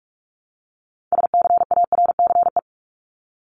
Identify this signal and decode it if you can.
Morse code, heard as a keyed tone, about 44 wpm, 700 Hz. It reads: SCARCE
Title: LY Suffix - Top 100 Words with Base Words 45wpm